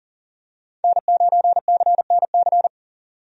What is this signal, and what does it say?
Morse code, heard as a keyed tone, about 40 wpm, 700 Hz. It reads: N9CNC
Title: USA Call Signs - 5 Characters - 40wpm